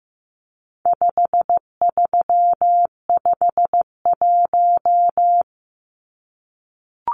Morse code, heard as a keyed tone, about 15 wpm, 700 Hz. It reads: 5351
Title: Numbers - 4-Digit 15wpm